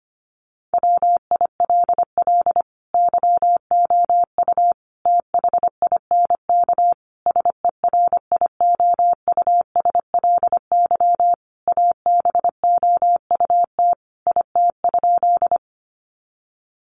Morse code, heard as a keyed tone, about 25 wpm, 700 Hz. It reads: WILL YOU THINK SERIOUSLY ABOUT IT?
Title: Sentences from Top 1200 Words 25wpm